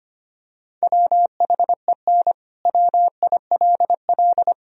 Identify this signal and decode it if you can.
Morse code, heard as a keyed tone, about 25 wpm, 690 Hz. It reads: WHEN WILL
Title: Sentences from Top 1000 Words - Review Part 4 - 25wpm